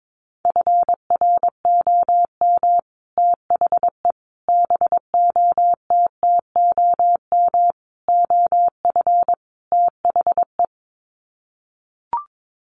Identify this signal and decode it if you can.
Morse code, heard as a keyed tone, about 22 wpm, 700 Hz. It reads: FROM THE BOTTOM OF THE
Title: Sets of 5 Words Encore2 22wpm